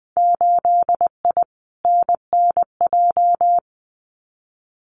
Morse code, heard as a keyed tone, about 20 wpm, 700 Hz. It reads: 8I NNJ
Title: Winter Field Day Exchanges Encore 20wpm